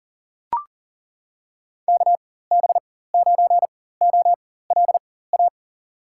Morse code, heard as a keyed tone, about 40 wpm, spaced as for 19 wpm, 700 Hz. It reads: KB9OLA
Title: USA Call Signs - 6 Characters - 4x Character Spacing - 40wpm